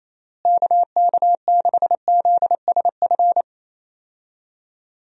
Morse code, heard as a keyed tone, about 28 wpm, 700 Hz. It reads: KK6ZSF
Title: USA Call Signs - 6 Characters - Repeated 2x - 28wpm